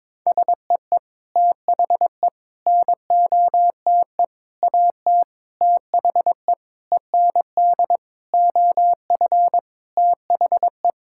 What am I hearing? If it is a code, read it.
Morse code, heard as a keyed tone, about 22 wpm, 700 Hz. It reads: SEE THE NOTE AT THE END OF THE
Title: Sentences from Top 1000 Words - Review Part 5 - 22wpm